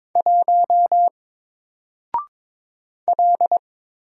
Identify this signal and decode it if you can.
Morse code, heard as a keyed tone, about 22 wpm, 700 Hz. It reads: 1 L
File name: Single-Letter-Number-22wpm